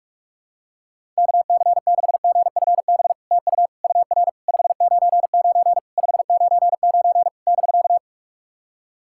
Morse code, heard as a keyed tone, about 44 wpm, 700 Hz. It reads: KC6GFB TU UR 599 599 <BK>
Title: SOTA QSO Practice 45wpm